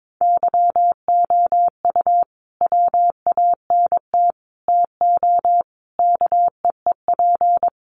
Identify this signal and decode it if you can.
Morse code, heard as a keyed tone, about 22 wpm, 700 Hz. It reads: YOU WANT TO KEEP
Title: Sentences from Top 1400 Words 22wpm